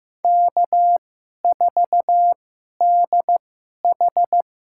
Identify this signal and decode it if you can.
Morse code, heard as a keyed tone, about 15 wpm, spaced as for 11 wpm, 700 Hz. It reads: K4DH
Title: USA Call Signs - 4 Characters - 2x Character Spacing - 15wpm